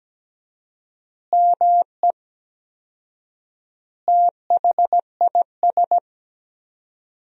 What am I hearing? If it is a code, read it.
Morse code, heard as a keyed tone, about 17 wpm, 700 Hz. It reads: ME THIS
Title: 4x Word-Spacing - Sentences from Top 500 Words - 17wpm